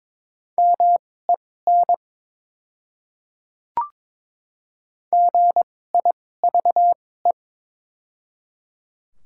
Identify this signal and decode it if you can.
Morse code, heard as a keyed tone, about 21 wpm, spaced as for 16 wpm, 700 Hz. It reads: MEN GIVE
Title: 2x Character Spacing - Top 200 Words - 22wpm